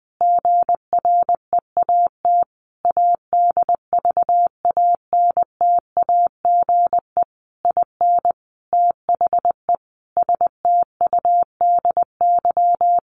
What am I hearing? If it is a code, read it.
Morse code, heard as a keyed tone, about 20 wpm, 700 Hz. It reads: GREAT ADVANTAGE IN THE STUDY
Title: Sentences from Top 900 Words 20wpm